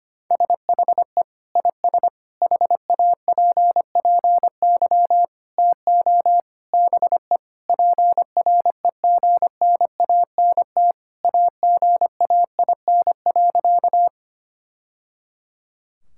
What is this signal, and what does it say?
Morse code, heard as a keyed tone, about 25 wpm, 700 Hz. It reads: SHE IS HAPPY TO BE PREGNANT AGAIN.
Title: Sentences from Top 1000 Words - Review Part 6 - 25wpm